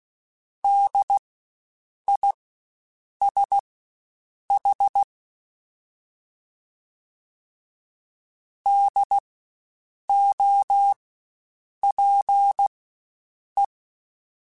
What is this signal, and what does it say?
Morse code, heard as a keyed tone, about 16 wpm, spaced as for 7 wpm, 800 Hz. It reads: DISH DOPE
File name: Spaced Alphas / Spaced Alphas mp3s / Ds sp @15